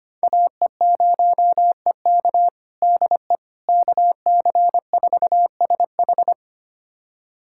Morse code, heard as a keyed tone, about 25 wpm, 700 Hz. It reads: AE0EK DE KC4SH
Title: USA Callsign DE Callsign - 5 Characters - 25wpm